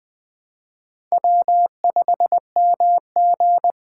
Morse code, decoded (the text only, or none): W5MG